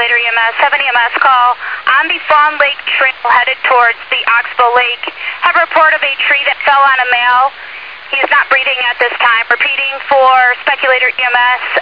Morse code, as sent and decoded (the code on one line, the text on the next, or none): none